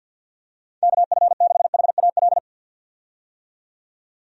Morse code, heard as a keyed tone, about 52 wpm, 700 Hz. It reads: KF6HAL
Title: USA Call Signs - 6 Characters - 50wpm